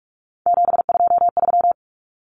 Morse code, heard as a keyed tone, about 44 wpm, 700 Hz. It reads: 723
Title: Numbers - 3-Digit 45wpm